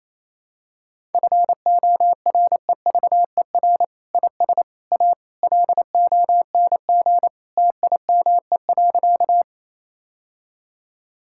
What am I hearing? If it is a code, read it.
Morse code, heard as a keyed tone, about 28 wpm, 700 Hz. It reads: FOREVER IS A LONG TIME.
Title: Sentences from Top 1600 Words 28wpm